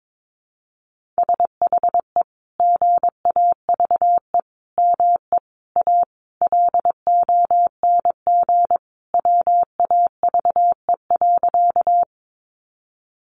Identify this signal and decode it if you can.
Morse code, heard as a keyed tone, about 22 wpm, 700 Hz. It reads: SHE GAVE ME A LONG WAVE.
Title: Sentences from Top 1400 Words 22wpm